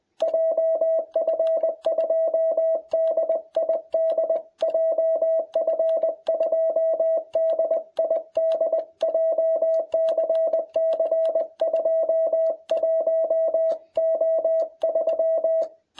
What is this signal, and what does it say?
Morse code, heard as a keyed tone, about 21 wpm, 660 Hz. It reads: JF2DID JF2DID JCC 21O3